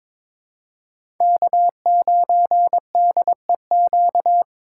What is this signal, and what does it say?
Morse code, heard as a keyed tone, about 22 wpm, 700 Hz. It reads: K9DEQ
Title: USA Call Signs - 5 Characters - 6x Character Spacing - 22wpm